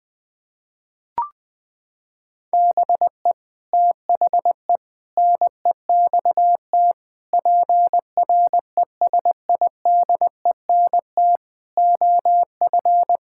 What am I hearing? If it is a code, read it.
Morse code, heard as a keyed tone, about 21 wpm, 700 Hz. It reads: BE THE NEXT PRESIDENT OF
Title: Sets of 8 Words - 20wpm